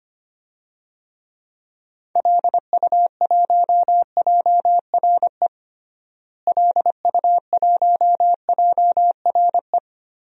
Morse code, heard as a keyed tone, about 25 wpm, 700 Hz. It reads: LU1JRE LU1JRE
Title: DXCC-100 Call Signs - 6 Characters - Repeated 3x - 25wpm